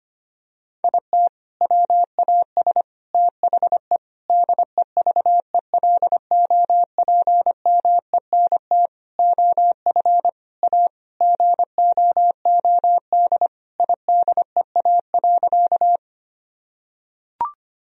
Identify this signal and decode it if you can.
Morse code, heard as a keyed tone, about 25 wpm, 700 Hz. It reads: IT WAS THE DEVELOPMENT OF A GOOD IDEA.
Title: Sentences from Top 1000 Words - Review Part 4 - 25wpm